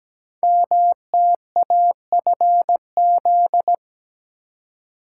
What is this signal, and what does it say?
Morse code, heard as a keyed tone, about 17 wpm, 700 Hz. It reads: MTAFZ
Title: Code Groups - 5 Letters - 17wpm